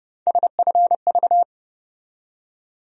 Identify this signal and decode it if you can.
Morse code, heard as a keyed tone, about 32 wpm, 700 Hz. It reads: SFV
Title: USA Call Sign Postfixes - 3 Letters 30wpm